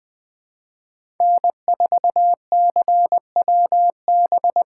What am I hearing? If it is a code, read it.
Morse code, heard as a keyed tone, about 20 wpm, 700 Hz. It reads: N4CWB
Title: 2023 POTA Hunter Call Signs - 20wpm